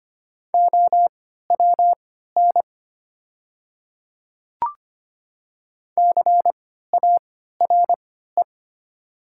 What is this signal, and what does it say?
Morse code, heard as a keyed tone, about 25 wpm, spaced as for 14 wpm, 700 Hz. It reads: OWN CARE
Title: 3x Character Spacing - Top 500 Words - 25wpm